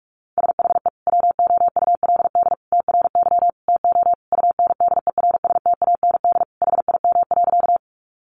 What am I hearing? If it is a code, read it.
Morse code, heard as a keyed tone, about 44 wpm, 700 Hz. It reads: SHE WOULD TRY TO UNDERSTAND HIM.
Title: Sentences from Top 600 Words 45wpm